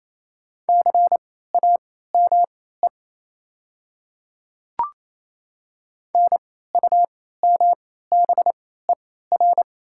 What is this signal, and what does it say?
Morse code, heard as a keyed tone, about 28 wpm, spaced as for 16 wpm, 700 Hz. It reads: CAME NUMBER
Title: 3x Character Spacing - Top 200 Words - 28wpm